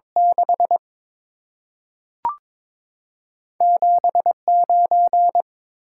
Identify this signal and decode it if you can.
Morse code, heard as a keyed tone, about 21 wpm, 700 Hz. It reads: 6 7 9
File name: practice / 22wpm / Two Letter-Number Combinations - 22wpm